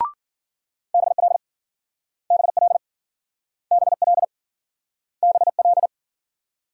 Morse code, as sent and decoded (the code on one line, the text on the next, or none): -... .-.. / -... .-.. / -... .-.. / -... .-..
BL BL BL BL